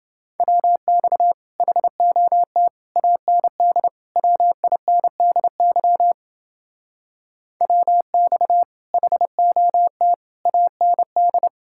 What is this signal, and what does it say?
Morse code, heard as a keyed tone, about 28 wpm, 700 Hz. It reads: WX HOT AND WINDY WX HOT AND